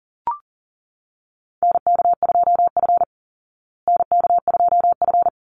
Morse code, heard as a keyed tone, about 37 wpm, 700 Hz. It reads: NK2F NK2F
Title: Call Signs Repeated 3x - 40wpm